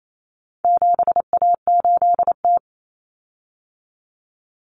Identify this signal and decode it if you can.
Morse code, heard as a keyed tone, about 28 wpm, 700 Hz. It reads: 7A8T
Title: Common Cut Numbers - 4-Digit 28wpm